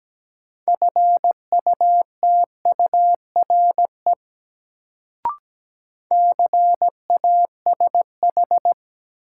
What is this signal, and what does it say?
Morse code, heard as a keyed tone, about 17 wpm, 700 Hz. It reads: FUTURE CASH
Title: Sentences from Top 700 Words - New Word Drill - 17wpm